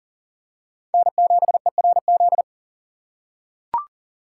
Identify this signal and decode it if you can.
Morse code, heard as a keyed tone, about 40 wpm, 700 Hz. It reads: N7ERZ
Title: USA Call Signs - 5 Characters - 2x Character Spacing - 40wpm